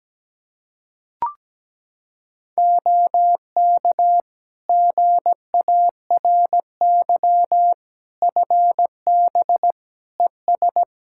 Morse code, heard as a keyed tone, about 17 wpm, 700 Hz. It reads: OK GARY FB ES
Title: Practice QSO - Exchange 2 - Info - 17wpm